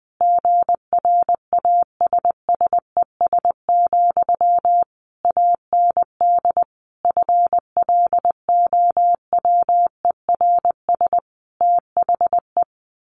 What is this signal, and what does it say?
Morse code, heard as a keyed tone, about 20 wpm, 700 Hz. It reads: GRASSES, AND FLOWERS THE
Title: A Sound Of Thunder 20wpm